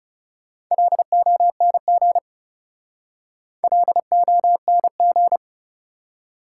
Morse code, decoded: LONG LONG

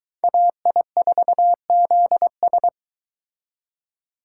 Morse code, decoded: AI4ZS

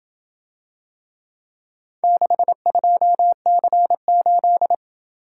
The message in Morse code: 62C8

